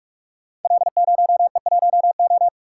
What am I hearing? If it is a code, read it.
Morse code, heard as a keyed tone, about 44 wpm, 700 Hz. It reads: L0E1O
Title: Code Groups - 5 Letter:Numbers - 45wpm